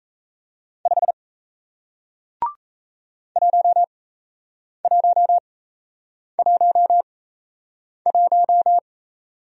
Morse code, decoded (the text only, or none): 5 1111